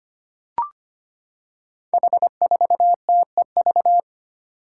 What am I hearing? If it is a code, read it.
Morse code, heard as a keyed tone, about 25 wpm, 700 Hz. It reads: H4TEV
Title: Five Challenging Letter-Number Combinations 25wpm